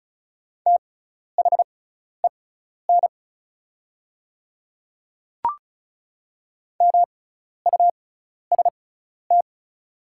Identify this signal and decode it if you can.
Morse code, heard as a keyed tone, about 36 wpm, spaced as for 12 wpm, 700 Hz. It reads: THEN MUST
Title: 6x Character Spacing - Top 200 Words - 35wpm